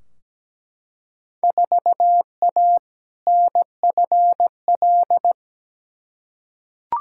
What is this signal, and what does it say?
Morse code, heard as a keyed tone, about 17 wpm, 700 Hz. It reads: 4A NFL
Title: Field Day Exchanges - 17wpm